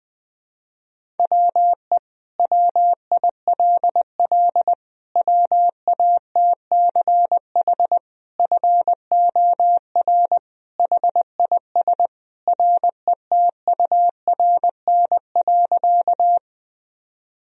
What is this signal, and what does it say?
Morse code, heard as a keyed tone, about 20 wpm, 700 Hz. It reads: WE WILL WATCH FOR HIS RETURN.
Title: Sentences from Top 1000 Words - Review Part 1 - 20wpm